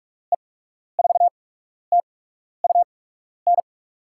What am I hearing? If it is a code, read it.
Morse code, heard as a keyed tone, about 45 wpm, spaced as for 12 wpm, 700 Hz. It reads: E4TUN
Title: USA Call Signs - 6 Characters - 8x Character Spacing - 45wpm